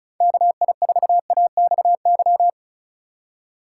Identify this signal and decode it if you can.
Morse code, heard as a keyed tone, about 36 wpm, 700 Hz. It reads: KI4AXY
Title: USA Call Signs - 6 Characters - 4x Character Spacing - 35wpm